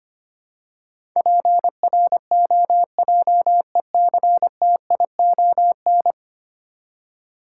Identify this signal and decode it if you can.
Morse code, heard as a keyed tone, about 25 wpm, 700 Hz. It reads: PROJECTION
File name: Ten Letter Words - 25wpm